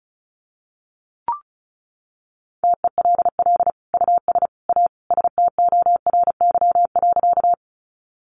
Morse code, heard as a keyed tone, about 34 wpm, 700 Hz. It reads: TELL US A STORY.
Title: Sentences from Top 300 Words 35wpm